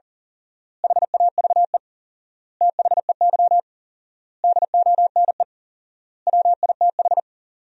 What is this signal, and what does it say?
Morse code, heard as a keyed tone, about 37 wpm, 700 Hz. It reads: HAVE THEY DONE WITH
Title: 4x Word-Spacing - Sentences from Top 500 Words - 40wpm